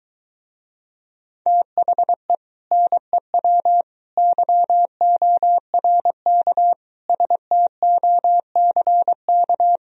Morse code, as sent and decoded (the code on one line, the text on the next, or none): - .... / -. .-- / -.-- --- .-. -.- / ... - --- -.-. -.-
THE NEW YORK STOCK